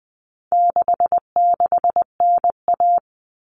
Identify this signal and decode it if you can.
Morse code, heard as a keyed tone, about 20 wpm, 700 Hz. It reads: 66NA
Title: All Cut Numbers - 4-Digit 20wpm